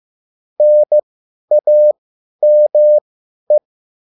Morse code, heard as a keyed tone, about 15 wpm, spaced as for 10 wpm, 600 Hz. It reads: NAME
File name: cwbook